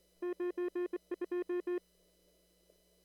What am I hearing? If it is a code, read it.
Morse code, heard as a keyed tone, about 23 wpm, 1060 Hz. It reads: 92
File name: CW / kt5j wpx cw-op-k5pi